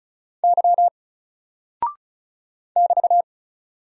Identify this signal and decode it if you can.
Morse code, heard as a keyed tone, about 36 wpm, 700 Hz. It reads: Y <BT>
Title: Single Character (Letter:Number:Punctuation:Prosign) 35wpm